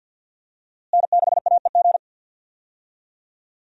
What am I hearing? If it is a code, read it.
Morse code, heard as a keyed tone, about 52 wpm, 700 Hz. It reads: N6AEG
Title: USA Call Signs - 5 Characters - Repeated 2x - 50wpm